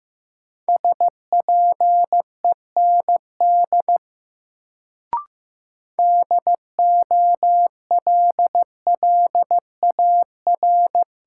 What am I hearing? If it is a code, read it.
Morse code, heard as a keyed tone, about 15 wpm, 700 Hz. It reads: SPEND DOLLAR
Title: Sets of 5 Words - New Word Drill - 15wpm